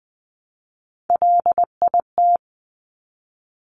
Morse code, heard as a keyed tone, about 20 wpm, 700 Hz. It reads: LIT
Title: Top Three Letter Combinations in Words 20wpm